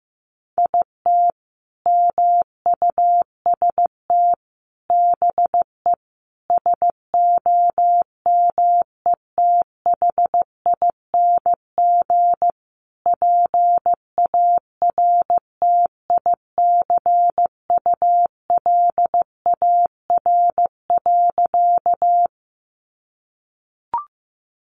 Morse code, heard as a keyed tone, about 15 wpm, 700 Hz. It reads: IT MUST BE SOMETHING PARTICULAR.
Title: Sentences from Top 600 Words 15wpm